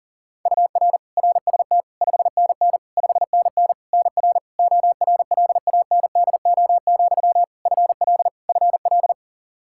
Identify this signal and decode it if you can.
Morse code, heard as a keyed tone, about 40 wpm, 700 Hz. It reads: UR RST 5NN 5NN NR ORLANDO, FL FL
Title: Practice QSO - Exchange 1 - Beginning - 40wpm